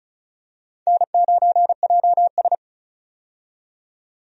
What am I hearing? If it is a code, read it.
Morse code, heard as a keyed tone, about 34 wpm, 700 Hz. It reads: N9JS
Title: USA Call Signs - 4 Characters - 35wpm